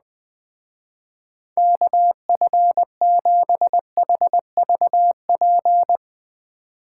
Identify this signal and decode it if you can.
Morse code, heard as a keyed tone, about 20 wpm, 700 Hz. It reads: KF7HVP
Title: Speed Racing - USA 2x3 Call Signs 17wpm